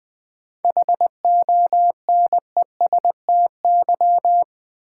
Morse code, heard as a keyed tone, about 20 wpm, 700 Hz. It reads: HONESTY
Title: 7 Letter Words - First 3 Letters Given - 20wpm